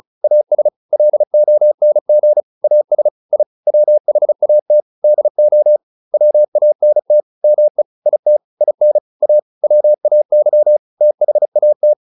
Morse code, heard as a keyed tone, about 34 wpm, 600 Hz. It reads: AS LONG AS I WHAT DO WANT ME IT IN A WAY THAT